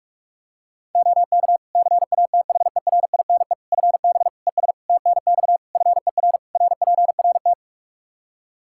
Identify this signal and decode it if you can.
Morse code, heard as a keyed tone, about 45 wpm, 700 Hz. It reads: OK CATHERINE FB ES TNX FER RPRT
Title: Practice QSO - Exchange 2 - Info - 45wpm